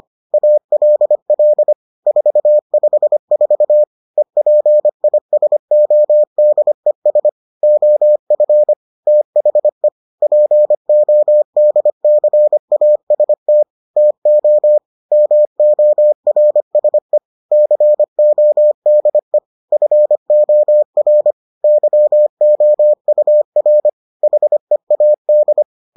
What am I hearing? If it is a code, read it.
Morse code, heard as a keyed tone, about 25 wpm, 600 Hz. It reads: ALL 454 EPISODES OF THE PODCAST TO MORSE CODE FOR YOUR HEAD